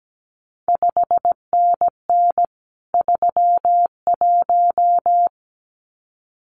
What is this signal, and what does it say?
Morse code, heard as a keyed tone, about 17 wpm, 700 Hz. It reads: 5NN 31
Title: CQ WW DX Contest Exchange - 17wpm